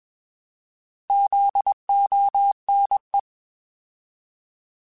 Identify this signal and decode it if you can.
Morse code, heard as a keyed tone, about 21 wpm, 800 Hz. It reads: ZONE
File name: Alphabetical / Alphabetical mp3s / Zs @20